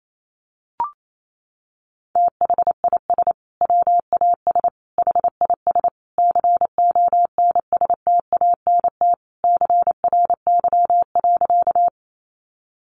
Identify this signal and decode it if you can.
Morse code, heard as a keyed tone, about 28 wpm, 700 Hz. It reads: THIS WAS HIS CONSTANT CRY.
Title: Sentences from Top 1500 Words 28wpm